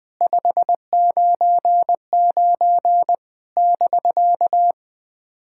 Morse code, heard as a keyed tone, about 20 wpm, 700 Hz. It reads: 599 <BK>
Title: SOTA QSO Practice 20wpm